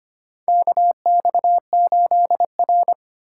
Morse code, heard as a keyed tone, about 25 wpm, 700 Hz. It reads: KX8R
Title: USA Call Signs - 4 Characters - 3x Character Spacing - 25wpm